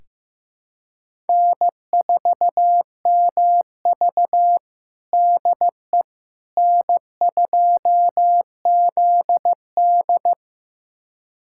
Morse code, heard as a keyed tone, about 15 wpm, 700 Hz. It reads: N4MV DE N2ZD
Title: USA Callsign DE Callsign - 4 Characters - 15wpm